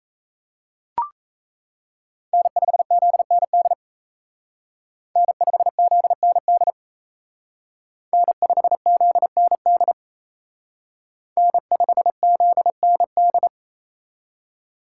Morse code, heard as a keyed tone, about 32 wpm, 700 Hz. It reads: N5ZND N5ZND N5ZND N5ZND
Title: Speed Racing - USA 1x3 Call Signs 28wpm